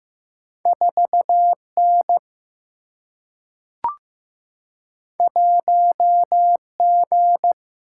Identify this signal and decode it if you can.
Morse code, heard as a keyed tone, about 15 wpm, 700 Hz. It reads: 4 N 1 G
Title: Single-Letter Single-Number 15wpm